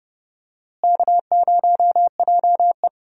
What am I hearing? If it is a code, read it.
Morse code, heard as a keyed tone, about 30 wpm, 700 Hz. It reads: K0JE
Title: USA Call Signs - 4 Characters - Repeated 2x - 30wpm